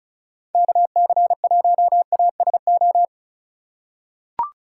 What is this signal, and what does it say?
Morse code, heard as a keyed tone, about 36 wpm, 700 Hz. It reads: KC1ASO
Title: USA Call Signs - 4x Character Spacing - 35wpm